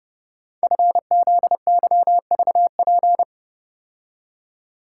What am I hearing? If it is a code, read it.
Morse code, heard as a keyed tone, about 30 wpm, 700 Hz. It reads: FZYVP
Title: Code Groups - 5 Letter:Numbers - 30wpm